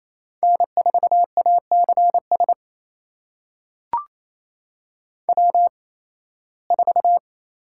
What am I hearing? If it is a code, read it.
Morse code, heard as a keyed tone, about 27 wpm, spaced as for 18 wpm, 700 Hz. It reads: N4ACS W4
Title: USA Call Signs - 5 Characters - 8x Character Spacing - 28wpm